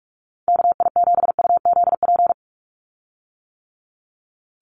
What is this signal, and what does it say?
Morse code, heard as a keyed tone, about 44 wpm, 700 Hz. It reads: KI7UZP
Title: USA Call Signs 45wpm